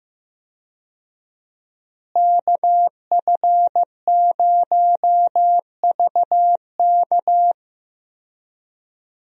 Morse code, heard as a keyed tone, about 16 wpm, 700 Hz. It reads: KF0VK
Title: USA Call Signs - 5 Characters - Repeated 3x - 15wpm